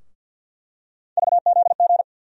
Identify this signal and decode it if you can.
Morse code, heard as a keyed tone, about 52 wpm, 700 Hz. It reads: VZG
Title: USA Call Sign Postfixes - 3 Letters 50wpm